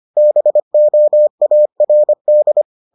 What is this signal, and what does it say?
Morse code, heard as a keyed tone, about 25 wpm, 600 Hz. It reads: BOARD